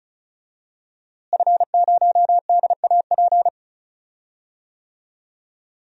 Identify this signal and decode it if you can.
Morse code, heard as a keyed tone, about 36 wpm, 700 Hz. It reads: F0DAP
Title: DXCC-100 Call Signs - 5 Characters - Repeated 2x - 35wpm